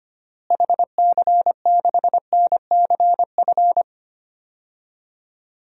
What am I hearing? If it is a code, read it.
Morse code, heard as a keyed tone, about 25 wpm, 700 Hz. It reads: HC6NCF
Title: DXCC-100 Call Signs - 6 Characters - Repeated 3x - 25wpm